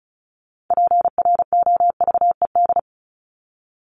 Morse code, heard as a keyed tone, about 34 wpm, 700 Hz. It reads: PROVED
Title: ED Suffix - Top 100 Words with Base Words 35wpm